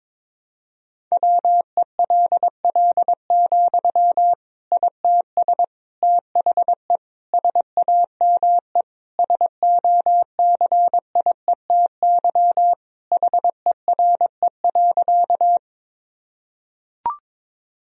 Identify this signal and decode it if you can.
Morse code, heard as a keyed tone, about 22 wpm, 700 Hz. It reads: WELL, ITS THE SAME SOCIETY HERE.
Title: Sentences from Top 1000 Words - Review Part 4 - 22wpm